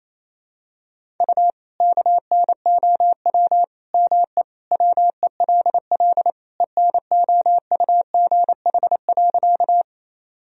Morse code, heard as a keyed tone, about 27 wpm, 700 Hz. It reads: U KNOW ME WELL ENOUGH.
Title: Sentences from Top 400 Words and Abbreviations 28wpm